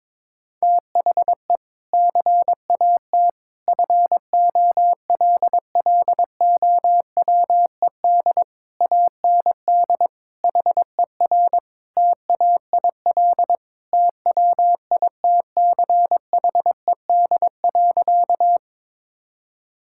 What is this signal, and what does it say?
Morse code, heard as a keyed tone, about 21 wpm, 700 Hz. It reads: THE CAT FOLLOWED AND HER TAIL TWITCHED.
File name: practice / 22wpm / Sentences with 2-Word Phrases - 22wpm